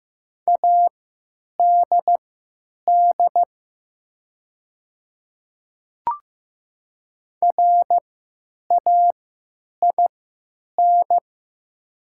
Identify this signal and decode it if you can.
Morse code, heard as a keyed tone, about 15 wpm, spaced as for 9 wpm, 700 Hz. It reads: ADD RAIN
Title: 3x Character Spacing - Top 500 Words - 15wpm